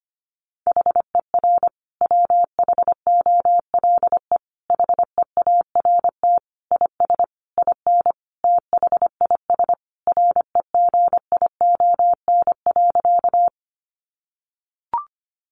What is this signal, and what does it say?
Morse code, heard as a keyed tone, about 25 wpm, 700 Hz. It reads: HER WHOLE HEART IS IN THIS REGION.
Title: Sentences from Top 1500 Words 25wpm